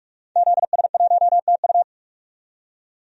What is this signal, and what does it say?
Morse code, heard as a keyed tone, about 42 wpm, 700 Hz. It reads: ZS1TU
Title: DXCC-100 Call Signs - 5 Characters - 45wpm